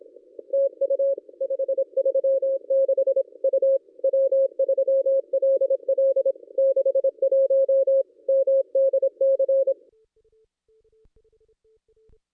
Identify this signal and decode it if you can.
Morse code, heard as a keyed tone, about 26 wpm, 540 Hz. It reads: TU 536 U W3LL 61 MDC
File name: contests / ss-cw-k5tr-op-n5rz -1400